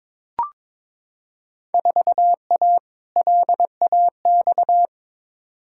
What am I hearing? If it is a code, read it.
Morse code, heard as a keyed tone, about 22 wpm, 700 Hz. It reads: 4A LAX
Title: Field Day Exchanges - 22wpm